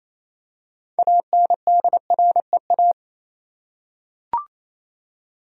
Morse code, heard as a keyed tone, about 28 wpm, 700 Hz. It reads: ANDREA
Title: USA Common Names - 28wpm